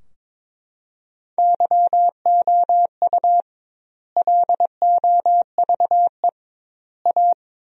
Morse code, heard as a keyed tone, about 22 wpm, 700 Hz. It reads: YOU LOVE A
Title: 2x Word-Spacing - Sentences from Top 500 Words - 22wpm